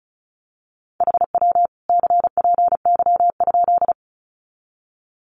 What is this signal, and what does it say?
Morse code, heard as a keyed tone, about 34 wpm, 700 Hz. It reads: HW CPY?
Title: Common QSO Phrases 35wpm